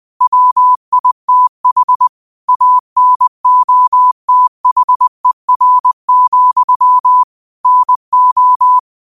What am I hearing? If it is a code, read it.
Morse code, heard as a keyed tone, about 20 wpm, 1000 Hz. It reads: WITH ANOTHER, NO
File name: qotdfiles / 20wpm